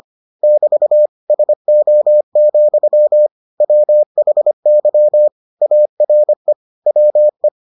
Morse code, heard as a keyed tone, about 25 wpm, 600 Hz. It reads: <BT> SO, WHY ARE WE